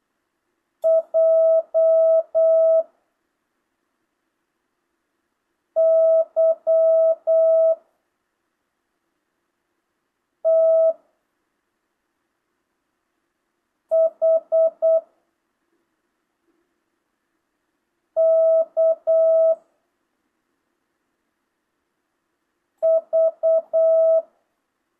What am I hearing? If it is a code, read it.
Morse code, heard as a keyed tone, about 8 wpm, 660 Hz. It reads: J Y T H K V